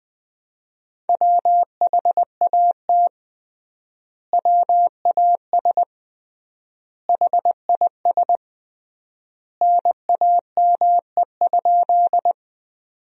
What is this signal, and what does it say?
Morse code, heard as a keyed tone, about 21 wpm, 700 Hz. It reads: WHAT WAS HIS NAME?
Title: 3x Word-Spacing - Sentences from Top 200 Words - 20wpm